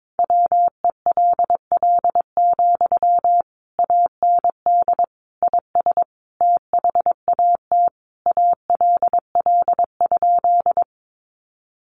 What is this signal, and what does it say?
Morse code, heard as a keyed tone, about 22 wpm, 700 Hz. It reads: WELL, AND IS THAT ALL?
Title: Sentences from Top 200 Words 22wpm